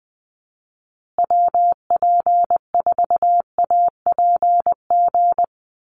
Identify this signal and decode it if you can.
Morse code, heard as a keyed tone, about 20 wpm, 700 Hz. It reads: WP4APG
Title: USA Call Signs Repeated 3x 20wpm